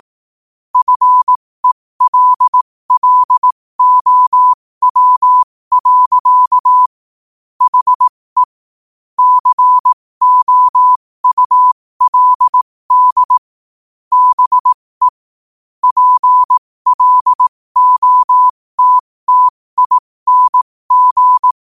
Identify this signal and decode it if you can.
Morse code, heard as a keyed tone, about 18 wpm, spaced as for 15 wpm, 1000 Hz. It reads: FELLOW. HE COULD BE PLOTTING
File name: qotdfiles / 15wpm